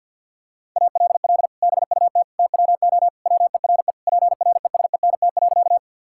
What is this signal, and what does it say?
Morse code, heard as a keyed tone, about 50 wpm, 700 Hz. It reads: ALL BUT TWO WERE PRESENT.
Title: Sentences from Top 1000 Words - Review Part 5 - 50wpm